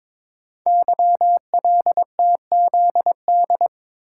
Morse code, heard as a keyed tone, about 22 wpm, 690 Hz. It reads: YLTZD
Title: Code Groups - 5 Letter:Numbers - 22wpm